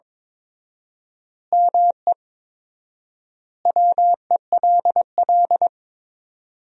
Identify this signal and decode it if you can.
Morse code, heard as a keyed tone, about 22 wpm, 700 Hz. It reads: ME WELL
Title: 4x Word-Spacing - Sentences from Top 400 Words - 22wpm